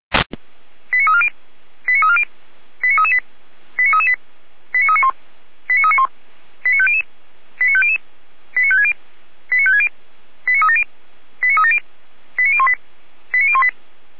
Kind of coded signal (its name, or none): five-tone selective call